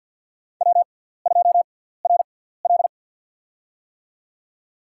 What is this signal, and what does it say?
Morse code, heard as a keyed tone, about 53 wpm, spaced as for 17 wpm, 700 Hz. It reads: W2RL
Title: USA Call Signs - 4 Characters - 6x Character Spacing - 50wpm